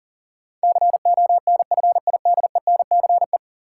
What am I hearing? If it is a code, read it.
Morse code, heard as a keyed tone, about 40 wpm, 700 Hz. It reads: CONFIDENCE
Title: Sentences from Top 1500 Words - New Word Drill Alt - 40wpm